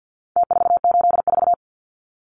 Morse code, heard as a keyed tone, about 49 wpm, 700 Hz. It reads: T484
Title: Numbers - Contest Style - 4-Digit 50wpm